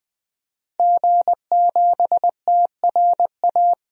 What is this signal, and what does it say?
Morse code, heard as a keyed tone, about 20 wpm, 700 Hz. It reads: G7TRA
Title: DXCC-100 Call Signs - 5 Characters - 20wpm